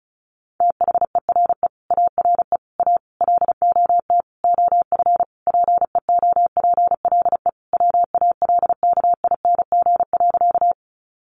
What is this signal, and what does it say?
Morse code, heard as a keyed tone, about 34 wpm, 700 Hz. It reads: THERE ARE A LOT OF PEOPLE WALKING.
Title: Easy Phrases 35wpm